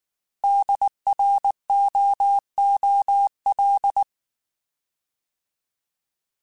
Morse code, heard as a keyed tone, about 19 wpm, 800 Hz. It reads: DROOL